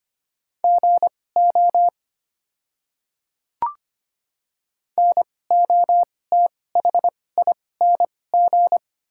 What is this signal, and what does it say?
Morse code, heard as a keyed tone, about 25 wpm, spaced as for 18 wpm, 690 Hz. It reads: GO NOTHING
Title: 2x Character Spacing - Top 500 Words - 25wpm